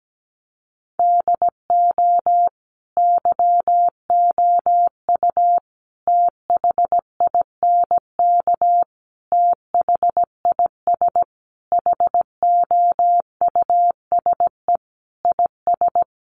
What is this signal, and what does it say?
Morse code, heard as a keyed tone, about 17 wpm, 700 Hz. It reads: DO YOU THINK THIS HOUSE IS
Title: Sentences from Top 1000 Words 17wpm